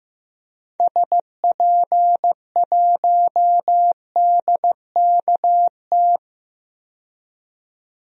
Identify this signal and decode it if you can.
Morse code, heard as a keyed tone, about 15 wpm, 700 Hz. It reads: SP1DKT
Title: DXCC-100 Call Signs - 6 Characters - 15wpm